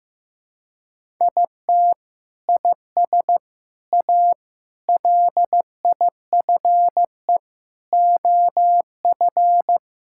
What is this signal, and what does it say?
Morse code, heard as a keyed tone, about 15 wpm, 700 Hz. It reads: IT IS A LIFE OF
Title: Sentences from Top 1000 Words - Review Part 5 - 15wpm